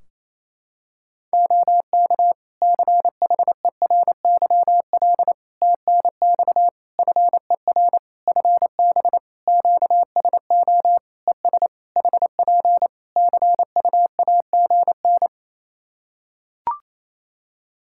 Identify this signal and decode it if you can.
Morse code, heard as a keyed tone, about 27 wpm, 700 Hz. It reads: OK CHERYL TNX FER FB QSO ES HP CUAGN
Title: Practice QSO - Exchange 4 - Ending - 28wpm